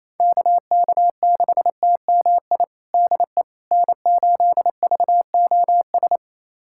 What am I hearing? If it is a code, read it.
Morse code, heard as a keyed tone, about 28 wpm, 700 Hz. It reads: KK6TMI DE N8VOS
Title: USA Callsign DE Callsign 28wpm